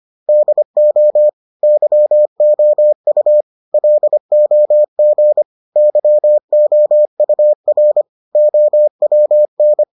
Morse code, heard as a keyed tone, about 25 wpm, 600 Hz. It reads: DO YOU LOG YOUR OWN